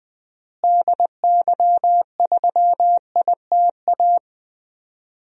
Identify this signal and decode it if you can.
Morse code, heard as a keyed tone, about 20 wpm, 700 Hz. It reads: DY3ITA
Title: DXCC-100 Call Signs - 6 Characters - Repeated 3x - 20wpm